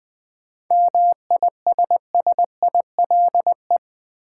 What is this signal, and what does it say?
Morse code, heard as a keyed tone, about 20 wpm, 690 Hz. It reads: MISSILE
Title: 7 Letter Words - First 3 Letters Given - 20wpm